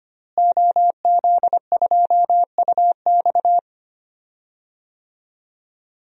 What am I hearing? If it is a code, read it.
Morse code, heard as a keyed tone, about 25 wpm, 700 Hz. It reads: OZ2UX
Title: DXCC-100 Call Signs - 5 Characters - Repeated 3x - 25wpm